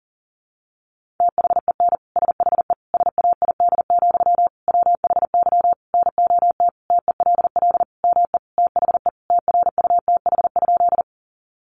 Morse code, heard as a keyed tone, about 40 wpm, 700 Hz. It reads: THEN SHE SAID, WHY NOT TELL ME THE TRUTH?
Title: Sentences from Top 800 Words 40wpm